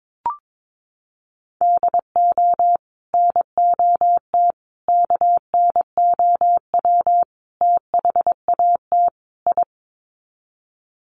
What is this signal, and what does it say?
Morse code, heard as a keyed tone, about 22 wpm, 700 Hz. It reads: DO NOT KNOW THAT I
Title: Sets of 5 Words Encore2 22wpm